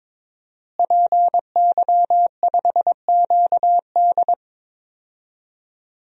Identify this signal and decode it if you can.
Morse code, heard as a keyed tone, about 22 wpm, 700 Hz. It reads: PY5QD
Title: DXCC-100 Call Signs - 5 Characters - Repeated 3x - 22wpm